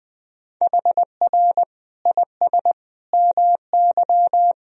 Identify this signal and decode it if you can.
Morse code, heard as a keyed tone, about 20 wpm, 700 Hz. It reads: HR IS MY
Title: Sentences from Top 200 Words and Abbreviations 20wpm